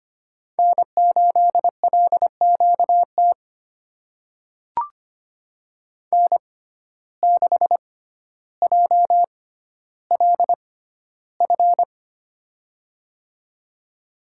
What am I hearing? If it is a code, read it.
Morse code, heard as a keyed tone, about 25 wpm, spaced as for 13 wpm, 700 Hz. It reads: N8LQT N6JLF
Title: USA Call Signs - 5 Characters - 6x Character Spacing - 25wpm